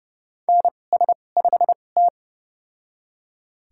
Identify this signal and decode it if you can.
Morse code, heard as a keyed tone, about 30 wpm, spaced as for 22 wpm, 700 Hz. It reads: NS5T
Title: USA Call Signs - 2x Character Spacing - 30wpm